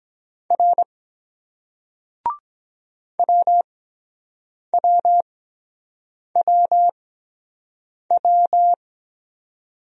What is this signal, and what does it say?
Morse code, heard as a keyed tone, about 22 wpm, spaced as for 7 wpm, 700 Hz. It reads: R WWWW